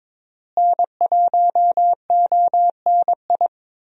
Morse code, heard as a keyed tone, about 22 wpm, 700 Hz. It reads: N1ONI
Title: USA Call Signs - 5 Characters - Repeated 3x - 22wpm